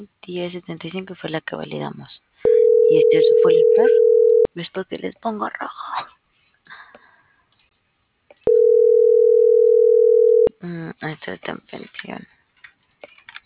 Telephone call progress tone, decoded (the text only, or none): ringback tone